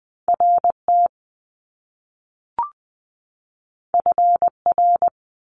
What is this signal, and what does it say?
Morse code, heard as a keyed tone, about 20 wpm, 700 Hz. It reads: RT FR
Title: Top Two Letter Combinations in Words 20wpm